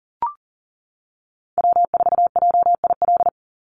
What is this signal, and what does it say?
Morse code, heard as a keyed tone, about 41 wpm, 700 Hz. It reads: W4JIL
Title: USA Call Signs 40wpm